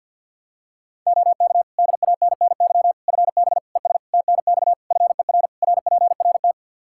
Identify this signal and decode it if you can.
Morse code, heard as a keyed tone, about 52 wpm, 700 Hz. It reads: OK DANNY FB ES TNX FER RPRT
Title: Practice QSO - Exchange 2 - Info - 50wpm